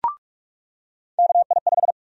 Morse code, heard as a keyed tone, about 44 wpm, 700 Hz. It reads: KI5
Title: USA Call Sign Prefixes - 45wpm